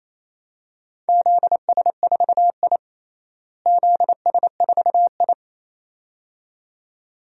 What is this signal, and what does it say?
Morse code, heard as a keyed tone, about 27 wpm, 700 Hz. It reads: ZS4I ZS4I
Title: DXCC-100 Call Signs - 4 Characters - Repeated 2x - 28wpm